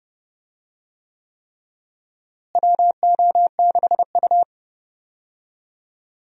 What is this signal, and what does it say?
Morse code, heard as a keyed tone, about 30 wpm, 700 Hz. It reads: WO6U
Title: USA Call Signs - 4 Characters - Repeated 3x - 30wpm